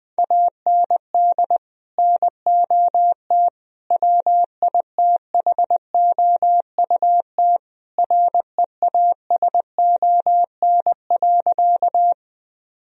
Morse code, heard as a keyed tone, about 20 wpm, 700 Hz. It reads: AND NOT WITHOUT REASON.
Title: Sentences from Top 1000 Words - Review Part 3 - 20wpm